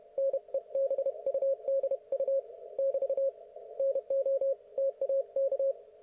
Morse code, heard as a keyed tone, about 31 wpm, 530 Hz. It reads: NEBUDU <BT> NO TAK
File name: Archiv / CUC107 / ok1fhi cul107